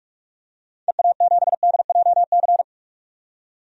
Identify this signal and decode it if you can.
Morse code, heard as a keyed tone, about 46 wpm, 700 Hz. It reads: EA7DJC
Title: DXCC-100 Call Signs - 6 Characters - 45wpm